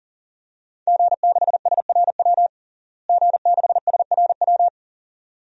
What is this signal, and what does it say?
Morse code, heard as a keyed tone, about 37 wpm, 700 Hz. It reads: G6SRW G6SRW
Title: DXCC-100 Call Signs - 5 Characters - Repeated 2x - 40wpm